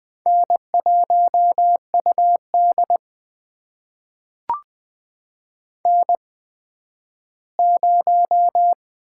Morse code, heard as a keyed tone, about 20 wpm, spaced as for 12 wpm, 700 Hz. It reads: N1UD N0
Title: USA Call Signs - 4 Characters - 8x Character Spacing - 20wpm